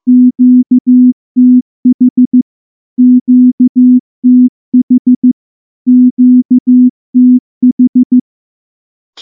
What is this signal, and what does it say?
Morse code, heard as a keyed tone, about 14 wpm, 250 Hz. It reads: QTH QTH QTH
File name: morse-training-15WPM